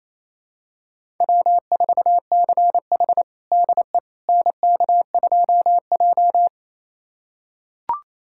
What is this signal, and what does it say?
Morse code, heard as a keyed tone, about 27 wpm, 700 Hz. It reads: W4CH DE NK2J
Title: USA Callsign DE Callsign - 4 Characters - 28wpm